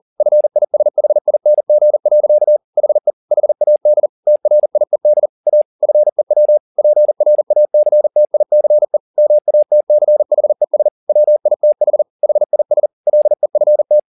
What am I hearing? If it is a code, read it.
Morse code, heard as a keyed tone, about 40 wpm, 600 Hz. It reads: FISHING. HE HAD TRIED A FEW PRACTICE MATCHES WITH HIS LEFT
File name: the old man and the sea 2 40wpm